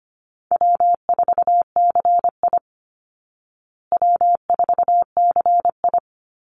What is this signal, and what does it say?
Morse code, heard as a keyed tone, about 25 wpm, 700 Hz. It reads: W4CI W4CI
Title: Call Signs Repeated 3x - 25wpm